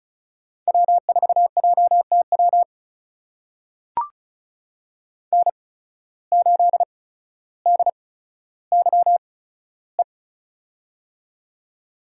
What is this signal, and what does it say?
Morse code, heard as a keyed tone, about 36 wpm, spaced as for 15 wpm, 700 Hz. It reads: W4JTW N8DYE
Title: USA Call Signs - 5 Characters - 8x Character Spacing - 35wpm